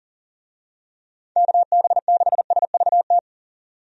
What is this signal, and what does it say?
Morse code, heard as a keyed tone, about 37 wpm, 700 Hz. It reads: KB6SVT
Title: USA Call Signs - 6 Characters - 40wpm